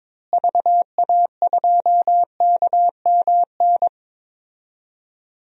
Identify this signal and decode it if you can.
Morse code, heard as a keyed tone, about 21 wpm, 700 Hz. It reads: VA2KMN
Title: DXCC-100 Call Signs - 6 Characters - Repeated 3x - 22wpm